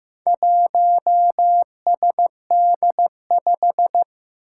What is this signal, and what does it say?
Morse code, heard as a keyed tone, about 15 wpm, 700 Hz. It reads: 1SD5
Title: Four Challenging Letter-Number Combinations 15wpm